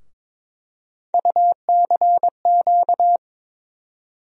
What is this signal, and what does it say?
Morse code, heard as a keyed tone, about 22 wpm, 700 Hz. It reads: UCQ
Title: USA Call Sign Postfixes - 3 Letters 22wpm